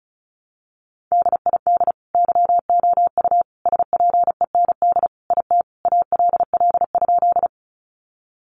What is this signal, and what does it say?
Morse code, heard as a keyed tone, about 34 wpm, 700 Hz. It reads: DID YOU SPEND IT ALL?
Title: Sentences from Top 800 Words 35wpm